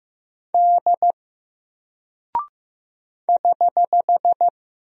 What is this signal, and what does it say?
Morse code, heard as a keyed tone, about 15 wpm, 700 Hz. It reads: D <HH>
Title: Single Character (Letter:Number:Punctuation:Prosign) 15wpm